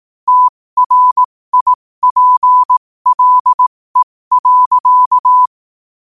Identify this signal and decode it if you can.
Morse code, heard as a keyed tone, about 18 wpm, spaced as for 15 wpm, 1000 Hz. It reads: TRIPLE.
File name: qotdfiles / 15wpm